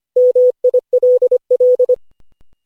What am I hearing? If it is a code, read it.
Morse code, heard as a keyed tone, about 25 wpm, 500 Hz. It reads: MILL
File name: cw / Session5 cw training